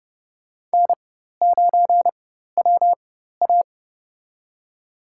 Morse code, heard as a keyed tone, about 32 wpm, spaced as for 14 wpm, 700 Hz. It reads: N9WA
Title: USA Call Signs - 4 Characters - 4x Character Spacing - 30wpm